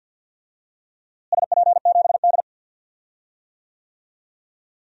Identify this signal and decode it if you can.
Morse code, heard as a keyed tone, about 52 wpm, 700 Hz. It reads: SP7D
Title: DXCC-100 Call Signs - 4 Characters - Repeated 2x - 50wpm